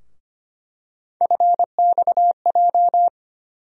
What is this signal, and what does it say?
Morse code, heard as a keyed tone, about 25 wpm, 700 Hz. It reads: FXJ
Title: USA Call Sign Postfixes - 3 Letters 25wpm